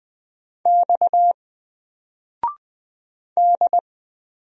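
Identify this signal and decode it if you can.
Morse code, heard as a keyed tone, about 20 wpm, 700 Hz. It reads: X D
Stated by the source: Single letter and number Morse drill at 20 wpm, from N4LCD.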